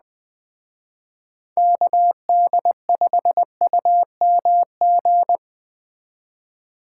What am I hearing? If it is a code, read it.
Morse code, heard as a keyed tone, about 20 wpm, 700 Hz. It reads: KD5UMG